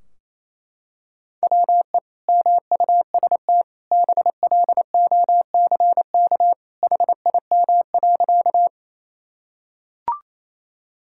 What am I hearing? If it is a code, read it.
Morse code, heard as a keyed tone, about 28 wpm, 700 Hz. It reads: WE MUST BLOCK HIM.
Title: Sentences from Top 1600 Words 28wpm